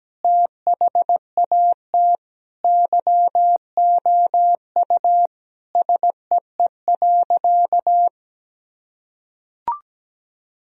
Morse code, heard as a keyed tone, about 17 wpm, 700 Hz. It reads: THAT YOU SEE.
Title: Sentences from Top 1000 Words - Review Part 2 - 17wpm